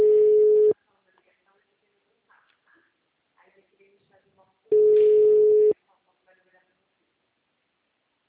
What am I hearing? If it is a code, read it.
Ringback tone (Europe, 425 Hz), with speech.
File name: out- -2011- - - .4799